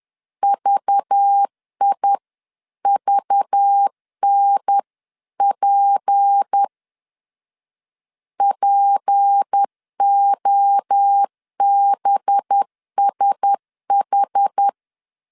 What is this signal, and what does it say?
Morse code, heard as a keyed tone, about 10 wpm, 800 Hz. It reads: VIVNP POBSH
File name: LEZIONI / 6 PKX0 / 6 P 10wpm 2432